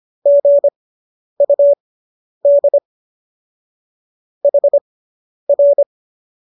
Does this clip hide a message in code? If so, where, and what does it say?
Yes: Morse code from 0.3 to 5.8 s, heard as a keyed tone, about 25 wpm, spaced as for 10 wpm, 570 Hz. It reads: GUD HR